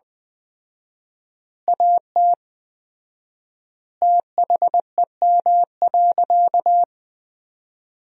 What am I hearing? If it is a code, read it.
Morse code, heard as a keyed tone, about 21 wpm, 700 Hz. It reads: AT THEM.
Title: 4x Word-Spacing - Sentences from Top 100 Words - 20wpm